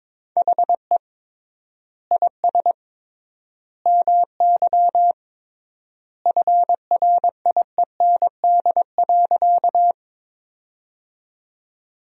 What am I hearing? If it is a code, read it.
Morse code, heard as a keyed tone, about 22 wpm, 700 Hz. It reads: HE IS MY FRIEND.
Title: 3x Word-Spacing - Sentences from Top 300 Words - 22wpm